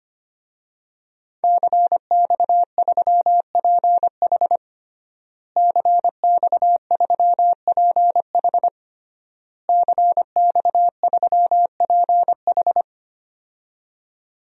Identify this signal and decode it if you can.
Morse code, heard as a keyed tone, about 25 wpm, 700 Hz. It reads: CX3PH CX3PH CX3PH
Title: DXCC-100 Call Signs - 5 Characters - Repeated 3x - 25wpm